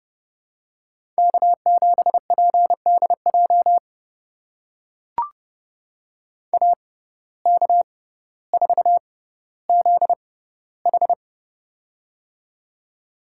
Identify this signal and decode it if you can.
Morse code, heard as a keyed tone, about 32 wpm, spaced as for 16 wpm, 700 Hz. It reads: K7PDJ AK4ZH
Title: USA Call Signs - 5 Characters - 6x Character Spacing - 30wpm